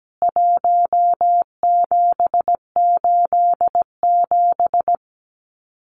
Morse code, heard as a keyed tone, about 17 wpm, 700 Hz. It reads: 1787
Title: CWT Exchange 17wpm